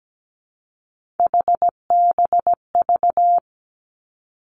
Morse code, heard as a keyed tone, about 17 wpm, 700 Hz. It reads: HBV
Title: Three Challenging Letter-Number Combinations 17wpm